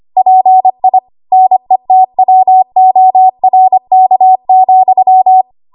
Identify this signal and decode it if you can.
Morse code, heard as a keyed tone, about 25 wpm, 750 Hz. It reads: PI NETWORK,
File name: Archive / 25 WPM / 25WPM